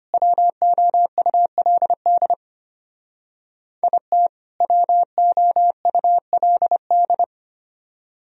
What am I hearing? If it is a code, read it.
Morse code, heard as a keyed tone, about 31 wpm, 700 Hz. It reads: WOULD IT WOULD